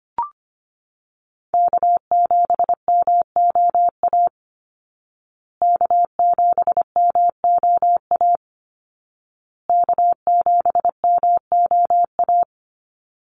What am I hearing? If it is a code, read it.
Morse code, heard as a keyed tone, about 25 wpm, 700 Hz. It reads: K7MOA K7MOA K7MOA
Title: Call Signs Repeated 2x - 25wpm